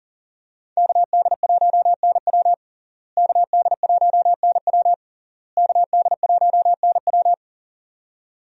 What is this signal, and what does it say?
Morse code, heard as a keyed tone, about 41 wpm, 700 Hz. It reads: KD1NW KD1NW KD1NW
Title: USA Call Signs - 5 Characters - Repeated 3x - 40wpm